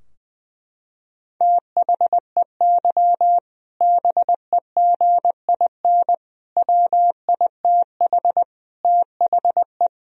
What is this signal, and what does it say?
Morse code, heard as a keyed tone, about 20 wpm, 700 Hz. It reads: THEY BEGIN WITH THE
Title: Sentences from Top 1000 Words - Review Part 3 - 20wpm